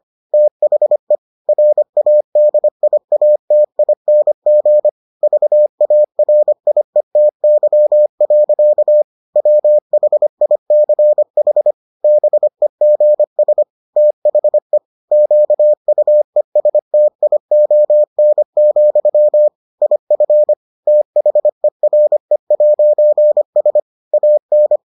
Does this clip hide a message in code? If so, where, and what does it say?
Yes: Morse code from 0.3 to 24.8 s, heard as a keyed tone, about 25 wpm, 600 Hz. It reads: THE RADIATING VARIETY. WHICH BEGS THE QUESTION, IF THERE'S AN